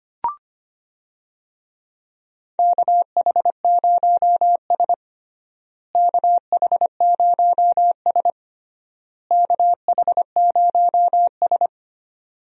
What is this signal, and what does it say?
Morse code, heard as a keyed tone, about 25 wpm, 700 Hz. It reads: KH0S KH0S KH0S
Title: USA Call Signs - 4 Characters - Repeated 3x - 25wpm